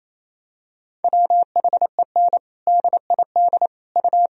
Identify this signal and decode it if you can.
Morse code, heard as a keyed tone, about 28 wpm, 700 Hz. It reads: WHEN DID U
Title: Sentences from Top 300 Words and Abbreviations 28wpm